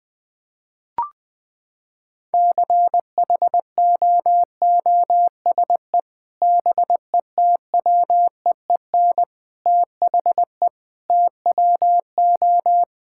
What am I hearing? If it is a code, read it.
Morse code, heard as a keyed tone, about 20 wpm, 700 Hz. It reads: CHOOSE BETWEEN THE TWO
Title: Sentences with 3-Word Phrases - 20wpm